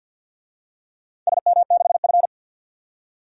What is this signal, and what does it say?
Morse code, heard as a keyed tone, about 49 wpm, 700 Hz. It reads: SM6F
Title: DXCC-100 Call Signs - 4 Characters - 50wpm